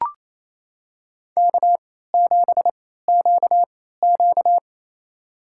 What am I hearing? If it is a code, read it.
Morse code, heard as a keyed tone, about 27 wpm, spaced as for 16 wpm, 700 Hz. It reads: K7QQ
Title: USA Call Signs - 3x Character Spacing - 28wpm